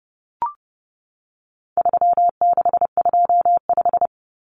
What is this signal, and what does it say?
Morse code, heard as a keyed tone, about 30 wpm, 700 Hz. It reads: 3625
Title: Common Cut Numbers - 4-Digit 30wpm